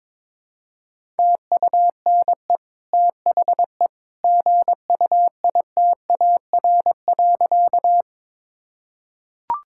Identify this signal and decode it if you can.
Morse code, heard as a keyed tone, about 22 wpm, 700 Hz. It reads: TUNE THE GUITAR.
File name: practice / 22wpm / Easy Sentences with Hints - 22wpm